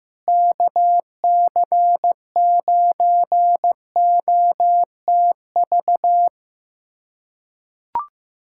Morse code, heard as a keyed tone, about 16 wpm, 700 Hz. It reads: KC9OTV
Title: USA Callsign DE Callsign 15wpm